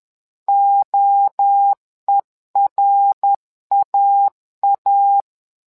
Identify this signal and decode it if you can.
Morse code, heard as a keyed tone, about 11 wpm, 800 Hz. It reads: OERAA
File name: LEZIONI / 3 MWRO / 3 MWRO 10wpm 2148